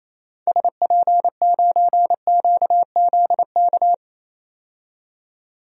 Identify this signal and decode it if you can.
Morse code, heard as a keyed tone, about 28 wpm, 700 Hz. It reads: SP9QZK
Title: DXCC-100 Call Signs - 6 Characters - 28wpm